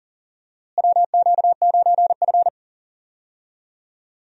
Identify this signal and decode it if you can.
Morse code, heard as a keyed tone, about 40 wpm, 700 Hz. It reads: WQ9F
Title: USA Call Signs - 4 Characters - Repeated 2x - 40wpm